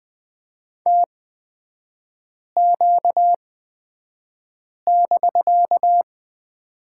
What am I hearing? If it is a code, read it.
Morse code, heard as a keyed tone, about 20 wpm, spaced as for 5 wpm, 700 Hz. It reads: TQ<BK>
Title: 250 Lesson 40e - BK - Three Characters 20wpm